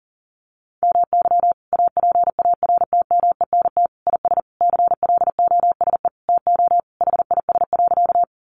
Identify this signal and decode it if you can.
Morse code, heard as a keyed tone, about 40 wpm, 700 Hz. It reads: MY APARTMENT IS CLOSE TO HIS.
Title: Sentences from Top 900 Words 40wpm